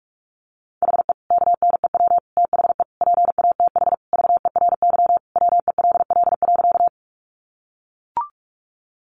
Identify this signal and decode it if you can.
Morse code, heard as a keyed tone, about 44 wpm, 700 Hz. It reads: HE KNEW THE PATH VERY WELL.
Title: Sentences from Top 1400 Words 45wpm